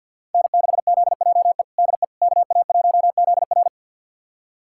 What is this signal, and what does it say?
Morse code, heard as a keyed tone, about 53 wpm, 700 Hz. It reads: N6ZJE DE KA1ZR
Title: USA Callsign DE Callsign - 5 Characters - 50wpm